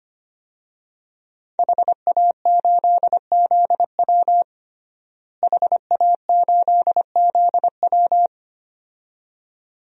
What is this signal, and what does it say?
Morse code, heard as a keyed tone, about 25 wpm, 700 Hz. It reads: HA8ZW HA8ZW
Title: DXCC-100 Call Signs - 5 Characters - Repeated 2x - 25wpm